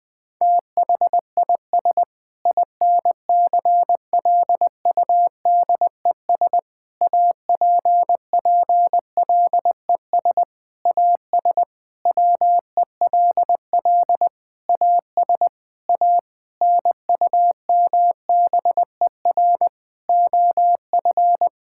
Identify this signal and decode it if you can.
Morse code, heard as a keyed tone, about 20 wpm, 700 Hz. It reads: THIS INCLUDES APPLES AS WELL AS A NUMBER OF
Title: Sentences with 7-Word Phrases - Spoken Beginning - 20wpm